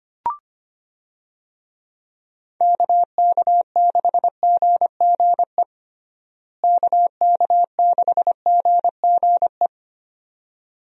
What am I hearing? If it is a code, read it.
Morse code, heard as a keyed tone, about 25 wpm, 700 Hz. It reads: KK6GGE KK6GGE
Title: USA Call Signs - 6 Characters - Repeated 2x - 25wpm